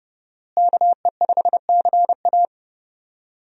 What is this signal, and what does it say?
Morse code, heard as a keyed tone, about 31 wpm, 700 Hz. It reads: KE5CA
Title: USA Call Signs - 5 Characters - Repeated 2x - 30wpm